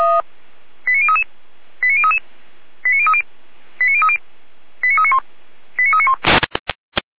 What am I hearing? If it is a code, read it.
Five-tone selective call (ZVEI), heard as quick runs of five tones, tones of about 70 ms. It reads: 89033 89033 89030 89030 89391 89391